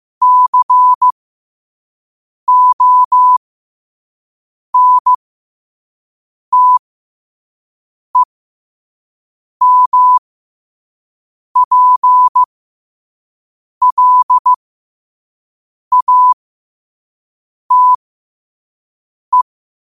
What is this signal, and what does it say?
Morse code, heard as a keyed tone, about 15 wpm, spaced as for 5 wpm, 1000 Hz. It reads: CONTEMPLATE
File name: qotdfiles / 5wpm